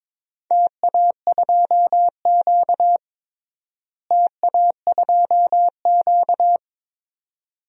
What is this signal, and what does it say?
Morse code, heard as a keyed tone, about 22 wpm, 700 Hz. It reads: TA2Q TA2Q
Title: DXCC-100 Call Signs - 4 Characters - Repeated 2x - 22wpm